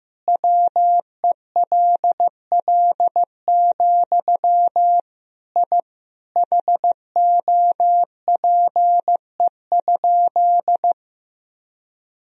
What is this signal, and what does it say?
Morse code, heard as a keyed tone, about 16 wpm, 700 Hz. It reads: WELL, I HOPE?
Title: Sentences from Top 1000 Words - Review Part 3 - 15wpm